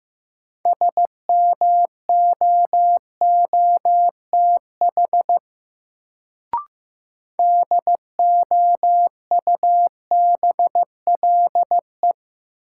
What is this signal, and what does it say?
Morse code, heard as a keyed tone, about 16 wpm, 700 Hz. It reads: SMOOTH DOUBLE
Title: Sentences from Top 1400 Words - New Word Drill Alt - 15wpm